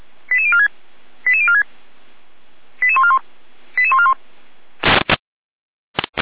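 Five-tone selective call (ZVEI), heard as quick runs of five tones, tones of about 80 ms. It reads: 88046 88046 88141 88141
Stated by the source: ZVEI selective call received by SDR.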